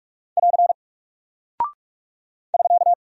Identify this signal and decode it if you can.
Morse code, heard as a keyed tone, about 45 wpm, 700 Hz. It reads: <AR> <SK>
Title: Single Character (Letter:Number:Punctuation:Prosign) Alt 45wpm